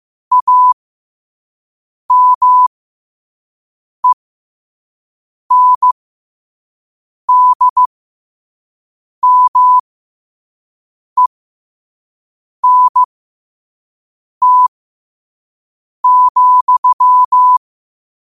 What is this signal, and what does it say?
Morse code, heard as a keyed tone, about 15 wpm, spaced as for 5 wpm, 1000 Hz. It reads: AMENDMENT,
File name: qotdfiles / 5wpm